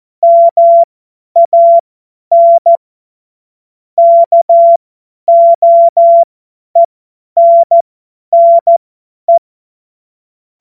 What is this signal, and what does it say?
Morse code, heard as a keyed tone, about 14 wpm, spaced as for 10 wpm, 680 Hz. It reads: MAN KOENNE
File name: Witz-08-Juli-2025-50-F70